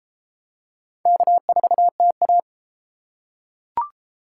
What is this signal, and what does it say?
Morse code, heard as a keyed tone, about 32 wpm, 700 Hz. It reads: K4TA